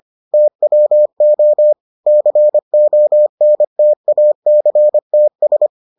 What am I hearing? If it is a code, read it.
Morse code, heard as a keyed tone, about 25 wpm, 600 Hz. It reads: TWO CONTACTS